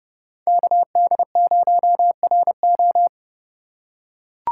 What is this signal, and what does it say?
Morse code, heard as a keyed tone, about 30 wpm, 700 Hz. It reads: KD0RO
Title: USA Call Signs - 5 Characters - 6x Character Spacing - 30wpm